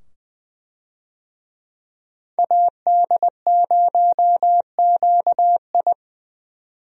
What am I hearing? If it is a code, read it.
Morse code, heard as a keyed tone, about 21 wpm, 700 Hz. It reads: AD0QI
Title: USA Call Signs - 5 Characters - 20wpm